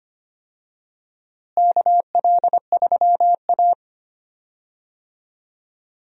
Morse code, heard as a keyed tone, about 25 wpm, 700 Hz. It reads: KL3A
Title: USA Call Signs - 4 Characters - Repeated 2x - 25wpm